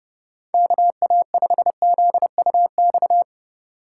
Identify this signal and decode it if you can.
Morse code, heard as a keyed tone, about 28 wpm, 700 Hz. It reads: KA5ZUX